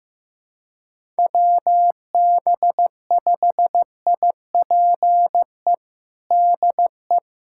Morse code, heard as a keyed tone, about 15 wpm, 700 Hz. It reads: WB5IPE DE
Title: USA Callsign DE Callsign 15wpm